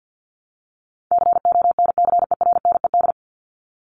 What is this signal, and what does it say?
Morse code, heard as a keyed tone, about 50 wpm, 700 Hz. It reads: CONCERNED
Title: ED Suffix - Top 100 Words with Base Words 50wpm